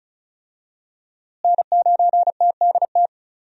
Morse code, heard as a keyed tone, about 36 wpm, 700 Hz. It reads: N9TDT
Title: USA Call Signs - 5 Characters - Repeated 3x - 35wpm